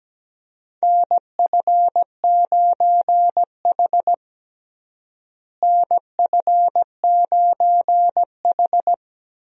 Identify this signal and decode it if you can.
Morse code, heard as a keyed tone, about 17 wpm, 700 Hz. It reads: NF9H NF9H
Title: USA Call Signs - 4 Characters - Repeated 2x - 17wpm